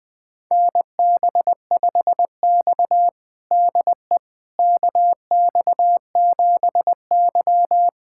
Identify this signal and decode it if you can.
Morse code, heard as a keyed tone, about 20 wpm, 700 Hz. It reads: NB5X DE KX7Y
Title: USA Callsign DE Callsign - 4 Characters - 20wpm